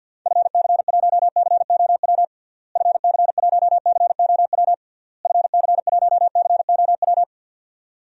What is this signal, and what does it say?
Morse code, heard as a keyed tone, about 52 wpm, 700 Hz. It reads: FC1COW FC1COW FC1COW
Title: DXCC-100 Call Signs - 6 Characters - Repeated 3x - 50wpm